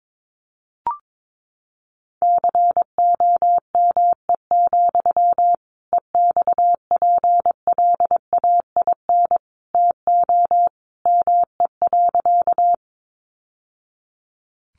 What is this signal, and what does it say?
Morse code, heard as a keyed tone, about 22 wpm, 700 Hz. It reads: COME, EXPLAIN TO ME.
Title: Sentences from Top 1100 Words 22wpm